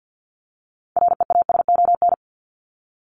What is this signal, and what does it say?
Morse code, heard as a keyed tone, about 50 wpm, 700 Hz. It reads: REASON
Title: ES:S Suffix - Top 100 Words with Base Words 50wpm